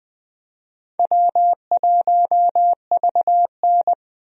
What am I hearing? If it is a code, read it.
Morse code, heard as a keyed tone, about 20 wpm, 700 Hz. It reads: W1VN
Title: USA Call Signs - 4 Characters - 20wpm